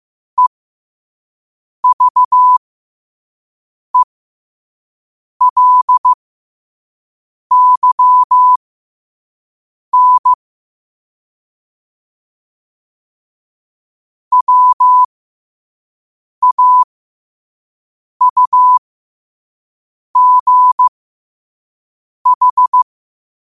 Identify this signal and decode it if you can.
Morse code, heard as a keyed tone, about 15 wpm, spaced as for 5 wpm, 1000 Hz. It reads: EVELYN WAUGH
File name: qotdfiles / 5wpm